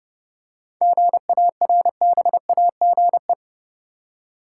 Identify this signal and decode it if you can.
Morse code, heard as a keyed tone, about 30 wpm, 700 Hz. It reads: GARBAGE
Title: Seven Letter Words - 30wpm